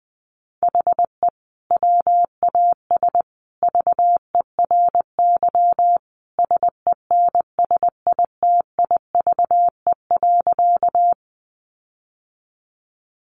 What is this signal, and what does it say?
Morse code, heard as a keyed tone, about 20 wpm, 700 Hz. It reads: HE WAS VERY SENSITIVE.
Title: Sentences from Top 1400 Words 20wpm